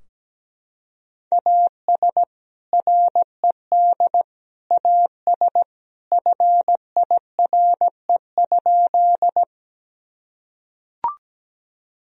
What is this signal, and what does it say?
Morse code, heard as a keyed tone, about 17 wpm, 700 Hz. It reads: AS RED AS FIRE?
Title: Sentences from Top 1000 Words - Review Part 2 - 17wpm